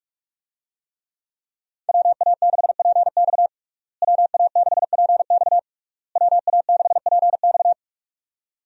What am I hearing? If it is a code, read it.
Morse code, heard as a keyed tone, about 46 wpm, 700 Hz. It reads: WA6PX WA6PX WA6PX
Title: USA Call Signs - 5 Characters - Repeated 3x - 45wpm